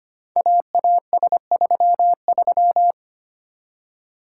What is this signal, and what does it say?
Morse code, heard as a keyed tone, about 25 wpm, 700 Hz. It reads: AAS33
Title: Code Groups - 5 Letter:Numbers - 25wpm